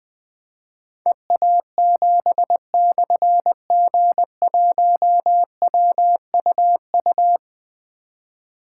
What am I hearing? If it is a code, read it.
Morse code, heard as a keyed tone, about 20 wpm, 700 Hz. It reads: EA7/G1WUU
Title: Difficult Call Signs 20wpm